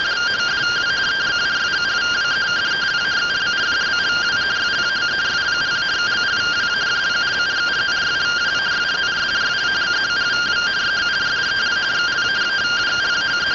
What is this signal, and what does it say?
Two-tone teleprinter signal, about 170 Hz shift.